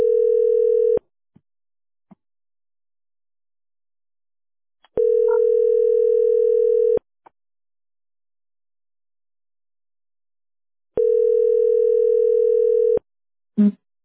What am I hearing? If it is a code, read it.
Ringback tone (North America), 440 + 480 Hz.